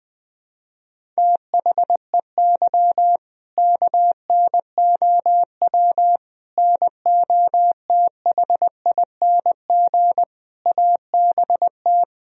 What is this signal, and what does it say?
Morse code, heard as a keyed tone, about 21 wpm, 700 Hz. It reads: THEY KNOW NOTHING ABT
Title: Sentences from Top 500 Words and Abbreviations 20wpm